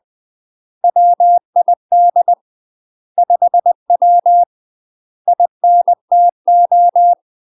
Morse code, heard as a keyed tone, about 20 wpm, 700 Hz. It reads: WID 5W INTO
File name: UK MASTER QSOs TRAINING / 20WPM / UK MASTER QSOs 0003